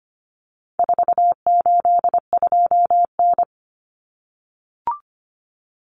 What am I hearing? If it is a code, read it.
Morse code, heard as a keyed tone, about 25 wpm, 700 Hz. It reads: 482N
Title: Common Cut Numbers - 4-Digit 25wpm